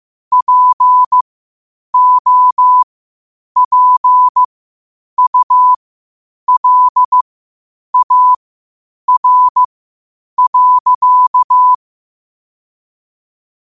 Morse code, heard as a keyed tone, about 15 wpm, spaced as for 8 wpm, 1000 Hz. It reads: POPULAR.
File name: qotdfiles / 8wpm